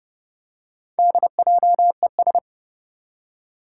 Morse code, heard as a keyed tone, about 31 wpm, 700 Hz. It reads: DJES
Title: Four Challenging Letter-Number Combinations 30wpm